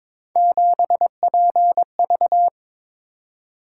Morse code, heard as a keyed tone, about 22 wpm, 700 Hz. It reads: 7PV
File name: practice / 22wpm / Three Letter-Number Combinations - 22wpm